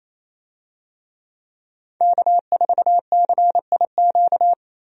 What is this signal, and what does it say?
Morse code, heard as a keyed tone, about 28 wpm, 700 Hz. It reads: K4CIQ
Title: USA Call Signs - 5 Characters - Repeated 3x - 28wpm